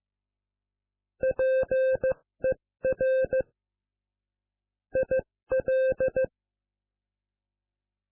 Morse code, heard as a keyed tone, about 15 wpm, 520 Hz. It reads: PER IL